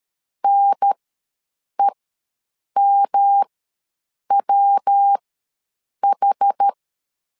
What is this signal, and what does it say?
Morse code, heard as a keyed tone, about 13 wpm, spaced as for 7 wpm, 800 Hz. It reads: NEMWH
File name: LEZIONI / 3 MWRO / 3 MW ws 12wpm 1816